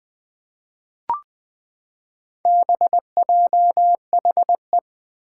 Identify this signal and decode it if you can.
Morse code, heard as a keyed tone, about 20 wpm, 700 Hz. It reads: BJHE
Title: Four Challenging Letter-Number Combinations 20wpm